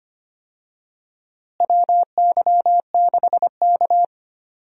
Morse code, heard as a keyed tone, about 25 wpm, 700 Hz. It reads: WY6K
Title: USA Call Signs - 4 Characters - Repeated 3x - 25wpm